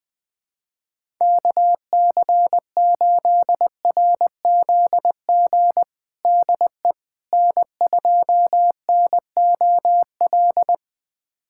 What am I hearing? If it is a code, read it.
Morse code, heard as a keyed tone, about 20 wpm, 700 Hz. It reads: KC8RZG DE N2NOL
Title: USA Callsign DE Callsign 20wpm